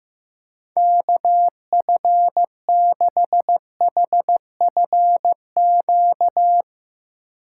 Morse code, heard as a keyed tone, about 15 wpm, 700 Hz. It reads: KF6HFQ
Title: USA Call Signs - 6 Characters - 8x Character Spacing - 15wpm